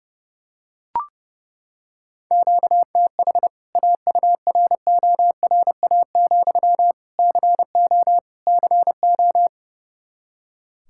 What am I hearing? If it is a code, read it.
Morse code, heard as a keyed tone, about 30 wpm, 700 Hz. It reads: QTH AURORA, CO CO
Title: K1USN SST QSO Practice 30wpm